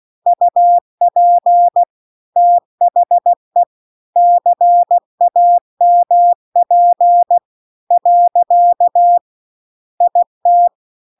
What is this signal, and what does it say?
Morse code, heard as a keyed tone, about 16 wpm, 700 Hz. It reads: UP THE CAMP . IT
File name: CW-The-Big-Twoheaded-River-2-16WPM